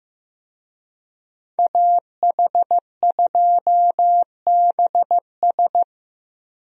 Morse code, heard as a keyed tone, about 16 wpm, 700 Hz. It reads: AH2BS
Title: USA Call Signs - 5 Characters - Repeated 3x - 15wpm